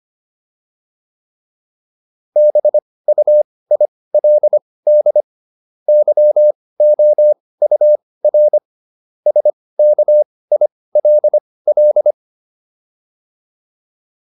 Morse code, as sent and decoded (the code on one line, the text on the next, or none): -... ..- .. .-.. -.. / -.-- --- ..- .-. / ... -.- .. .-.. .-..
BUILD YOUR SKILL